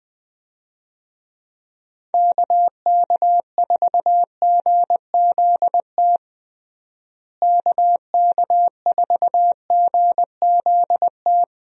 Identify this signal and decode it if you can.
Morse code, heard as a keyed tone, about 20 wpm, 700 Hz. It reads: KK4GZT KK4GZT
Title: USA Call Signs - 6 Characters - Repeated 3x - 20wpm